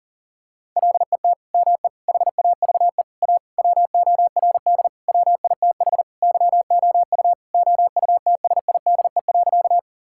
Morse code, heard as a keyed tone, about 37 wpm, 700 Hz. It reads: LET ME HAVE A WORD WITH YOU OUTSIDE.
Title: Sentences from Top 1000 Words - Review Part 6 - 40wpm